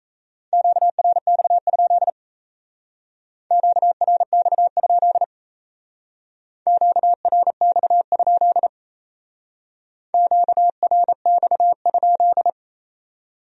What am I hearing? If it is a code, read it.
Morse code, heard as a keyed tone, about 37 wpm, 700 Hz. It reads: QRX? QRX? QRX? QRX?